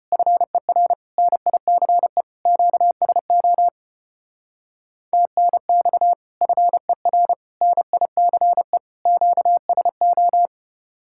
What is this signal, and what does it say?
Morse code, heard as a keyed tone, about 32 wpm, 700 Hz. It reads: FER NICE QSO TNX FER NICE QSO